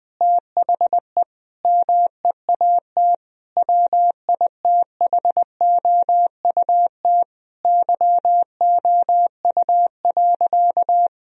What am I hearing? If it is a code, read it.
Morse code, heard as a keyed tone, about 20 wpm, 700 Hz. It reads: THE MEAT WITHOUT YOU.
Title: Sentences from Top 1000 Words - Review Part 3 - 20wpm